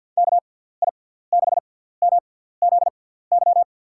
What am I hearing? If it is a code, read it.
Morse code, heard as a keyed tone, about 50 wpm, spaced as for 17 wpm, 700 Hz. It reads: KI6MZY
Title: USA Call Signs - 6 Characters - 6x Character Spacing - 50wpm